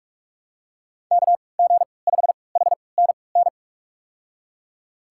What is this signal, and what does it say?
Morse code, heard as a keyed tone, about 45 wpm, spaced as for 25 wpm, 700 Hz. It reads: KG5HNN
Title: USA Call Signs - 6 Characters - 3x Character Spacing - 45wpm